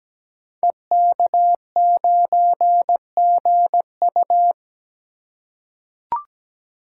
Morse code, heard as a keyed tone, about 17 wpm, 700 Hz. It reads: EK9GU
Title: Code Groups - 5 Letter:Numbers - 17wpm